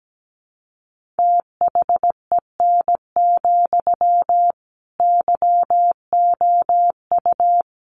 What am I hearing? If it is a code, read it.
Morse code, heard as a keyed tone, about 17 wpm, 700 Hz. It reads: THEN, YOU
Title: Sentences from Top 1500 Words 17wpm